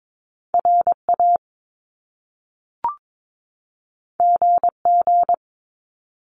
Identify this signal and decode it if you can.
Morse code, heard as a keyed tone, about 22 wpm, 700 Hz. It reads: RA GG
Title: Top Two Letter Combinations in Words 22wpm